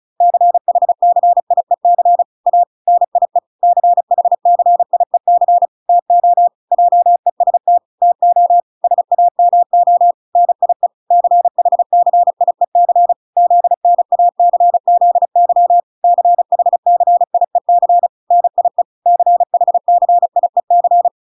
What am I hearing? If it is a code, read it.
Morse code, heard as a keyed tone, about 35 wpm, 700 Hz. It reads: CHCIEC A NIE CHCIEC TO JEST TO SAMO NIE CHCIEC ZNACZY CHCIEC NIE CHCIEC
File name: 35wpm